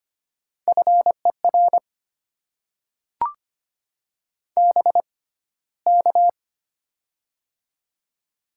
Morse code, heard as a keyed tone, about 25 wpm, spaced as for 15 wpm, 690 Hz. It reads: FER BK
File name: Top 90 QSO Elements - 6x Character Spacing - 25wpm